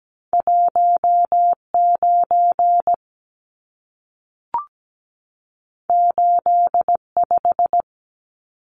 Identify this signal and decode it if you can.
Morse code, heard as a keyed tone, about 17 wpm, 700 Hz. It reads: 19 85
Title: Numbers - 2-Digit 17wpm